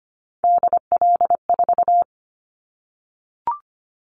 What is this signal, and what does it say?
Morse code, heard as a keyed tone, about 25 wpm, 700 Hz. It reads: DL4
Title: Callsign Prefixes for Top 25 DXCC - 25wpm